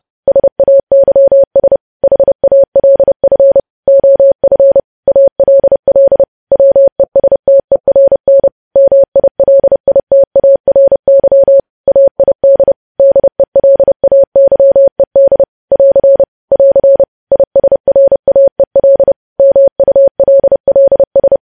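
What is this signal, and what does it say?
Morse code, heard as a keyed tone, about 30 wpm, 550 Hz. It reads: SAYS HALF OF ALL WESTERN MILITARY AID DELAYED <AR> <AR> ISRAEL MULLS
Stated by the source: Morse news headlines sent at 30 wpm, opening with vvv.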